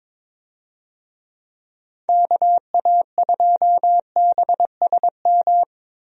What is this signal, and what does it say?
Morse code, heard as a keyed tone, about 22 wpm, 700 Hz. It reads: KA2BSM
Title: USA Call Signs - 6 Characters - 22wpm